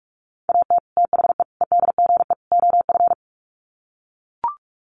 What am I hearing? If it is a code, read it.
Morse code, heard as a keyed tone, about 44 wpm, 700 Hz. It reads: AT THE EDGE OF
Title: Sets of 4 Words 45wpm